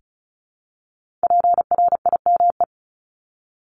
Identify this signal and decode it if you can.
Morse code, heard as a keyed tone, about 36 wpm, 700 Hz. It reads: PRIME
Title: Multi-letter Phonemes Repeated 3x 35wpm